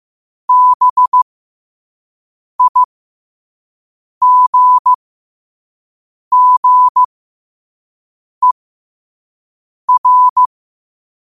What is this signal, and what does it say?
Morse code, heard as a keyed tone, about 15 wpm, spaced as for 5 wpm, 1000 Hz. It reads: BIGGER
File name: qotdfiles / 5wpm